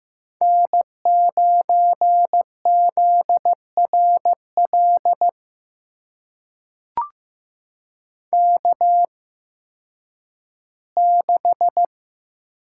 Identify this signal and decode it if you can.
Morse code, heard as a keyed tone, about 15 wpm, spaced as for 10 wpm, 700 Hz. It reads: N9ZRL K6
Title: USA Call Signs - 5 Characters - 8x Character Spacing - 15wpm